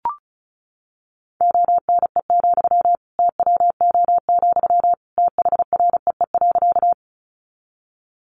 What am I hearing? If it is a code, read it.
Morse code, heard as a keyed tone, about 34 wpm, 700 Hz. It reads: ONE, TWO, THREE.
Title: Sentences from Top 200 Words 35wpm